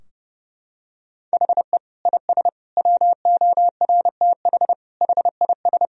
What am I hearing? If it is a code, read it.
Morse code, heard as a keyed tone, about 30 wpm, 700 Hz. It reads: HE IS WORTH HIS
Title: Sentences from Top 1000 Words - Review Part 3 - 30wpm